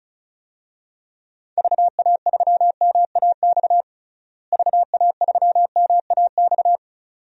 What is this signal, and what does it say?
Morse code, heard as a keyed tone, about 34 wpm, 700 Hz. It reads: VA3MAX VA3MAX
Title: DXCC-100 Call Signs - 6 Characters - Repeated 2x - 35wpm